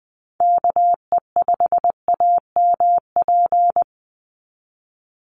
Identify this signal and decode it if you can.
Morse code, heard as a keyed tone, about 20 wpm, 700 Hz. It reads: KE5AMP
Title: USA Call Signs 20wpm